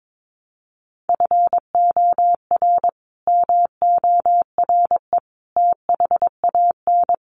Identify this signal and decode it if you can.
Morse code, heard as a keyed tone, about 22 wpm, 700 Hz. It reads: FOR MORE THAN
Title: Sets of 3 Words 22wpm